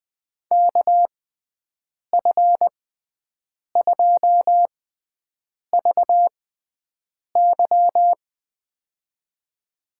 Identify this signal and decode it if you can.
Morse code, heard as a keyed tone, about 20 wpm, spaced as for 7 wpm, 700 Hz. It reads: KF2VY
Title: USA Call Signs - 6x Character Spacing - 20wpm